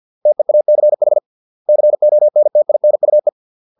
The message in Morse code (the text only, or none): TEACH CONTINUE